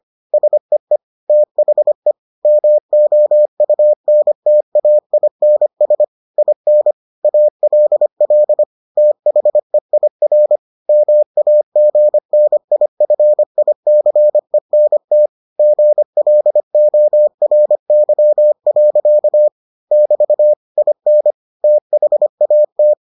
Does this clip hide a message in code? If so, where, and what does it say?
Yes: Morse code from 0.3 to 22.9 s, heard as a keyed tone, about 25 wpm, 600 Hz. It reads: SEE THE MOUNTAINS IN ALL THEIR MAGNIFICENT GLORY. <BT> IN THAT